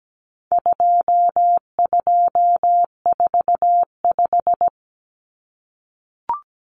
Morse code, heard as a keyed tone, about 17 wpm, 700 Hz. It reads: 2245
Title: Numbers - 4-Digit 17wpm